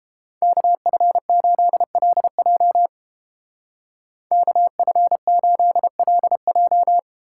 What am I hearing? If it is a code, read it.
Morse code, heard as a keyed tone, about 32 wpm, 700 Hz. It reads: KF8LJ KF8LJ